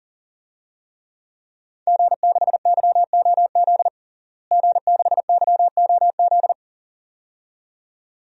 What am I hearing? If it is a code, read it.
Morse code, heard as a keyed tone, about 40 wpm, 700 Hz. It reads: G6YOZ G6YOZ
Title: DXCC-100 Call Signs - 5 Characters - Repeated 2x - 40wpm